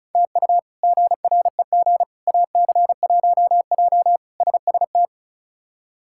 Morse code, heard as a keyed tone, about 34 wpm, 700 Hz. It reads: TU GREG AC1J SST
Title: K1USN SST QSO Practice - 35wpm